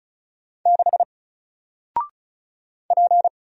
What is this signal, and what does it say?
Morse code, heard as a keyed tone, about 36 wpm, 700 Hz. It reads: 6 P
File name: Single-Letter-Number-35wpm